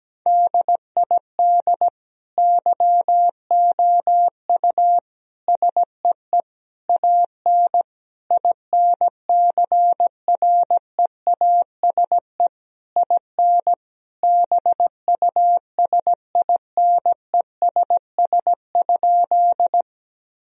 Morse code, heard as a keyed tone, about 17 wpm, 700 Hz. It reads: DID YOU SEE AN INCREASE IN BUSINESS?
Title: Sentences from Top 1000 Words - Review Part 4 - 17wpm